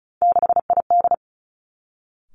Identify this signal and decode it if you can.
Morse code, heard as a keyed tone, about 36 wpm, 700 Hz. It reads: 6ID
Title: Three Challenging Letter-Number Combinations 35wpm